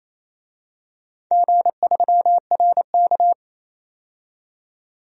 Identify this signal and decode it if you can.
Morse code, heard as a keyed tone, about 27 wpm, 700 Hz. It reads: G3RK
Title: DXCC-100 Call Signs - 4 Characters - 28wpm